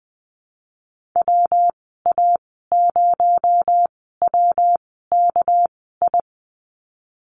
Morse code, heard as a keyed tone, about 20 wpm, spaced as for 15 wpm, 700 Hz. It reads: WA0WKI
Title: 219 Lesson 35g - Number 0 - Call Signs 20wpm